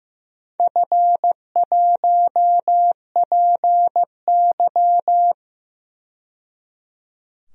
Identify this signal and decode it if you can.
Morse code, heard as a keyed tone, about 15 wpm, 700 Hz. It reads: F1PY
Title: DXCC-100 Call Signs - 4 Characters - 15wpm